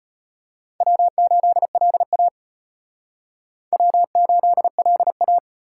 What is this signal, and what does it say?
Morse code, heard as a keyed tone, about 33 wpm, 700 Hz. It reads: W8LA W8LA